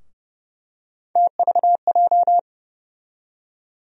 Morse code, heard as a keyed tone, about 29 wpm, 700 Hz. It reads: TVJ
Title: USA Call Sign Postfixes - 3 Letters 30wpm